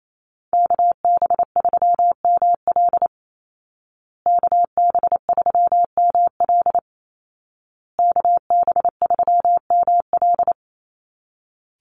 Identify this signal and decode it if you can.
Morse code, heard as a keyed tone, about 28 wpm, 700 Hz. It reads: KB3ML KB3ML KB3ML
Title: Call Signs Repeated 2x - 28wpm